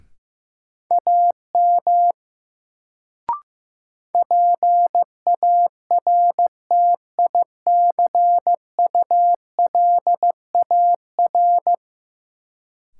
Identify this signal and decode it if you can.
Morse code, heard as a keyed tone, about 15 wpm, 700 Hz. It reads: AM PARTICULAR
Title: Sentences from Top 600 Words - New Word Drill - 15wpm